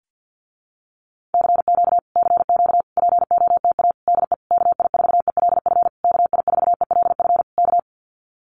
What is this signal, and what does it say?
Morse code, heard as a keyed tone, about 50 wpm, 700 Hz. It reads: CQ CQ POTA DE KI4ELF KI4ELF K
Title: Common QSO Phrases Encore 50wpm